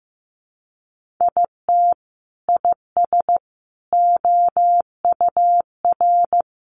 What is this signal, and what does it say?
Morse code, heard as a keyed tone, about 15 wpm, 700 Hz. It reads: IT IS OUR
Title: Sentences from Top 1500 Words 15wpm